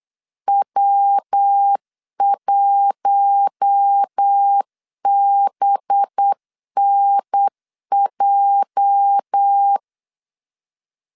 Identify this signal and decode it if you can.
Morse code, heard as a keyed tone, about 8 wpm, 800 Hz. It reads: W1BNJ
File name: LEZIONI / 8 YJ45 / 8 YJ 8wpm 3313